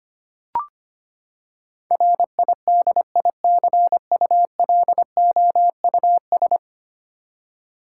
Morse code, heard as a keyed tone, about 25 wpm, 700 Hz. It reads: RIDICULOUS
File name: Ten Letter Words - 25wpm